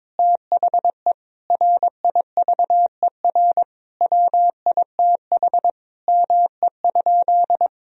Morse code, heard as a keyed tone, about 22 wpm, 700 Hz. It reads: THE RIVER WITH ME?
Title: Sentences from Top 1000 Words - Review Part 3 - 22wpm